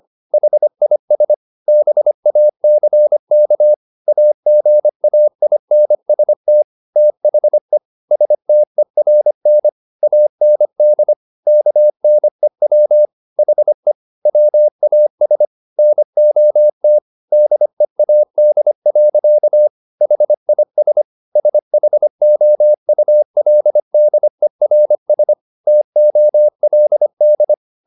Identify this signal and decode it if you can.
Morse code, heard as a keyed tone, about 25 wpm, 600 Hz. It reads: HIS BACK AGAINST THE STERN AND KNEW HE WAS NOT DEAD. HIS SHOULDERS TOLD